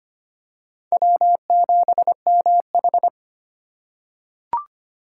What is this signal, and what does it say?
Morse code, heard as a keyed tone, about 25 wpm, 700 Hz. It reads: W7MH
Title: USA Call Signs - 4 Characters - 3x Character Spacing - 25wpm